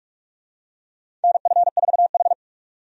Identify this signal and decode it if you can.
Morse code, heard as a keyed tone, about 46 wpm, 700 Hz. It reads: NF4H
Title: USA Call Signs - 4 Characters - 45wpm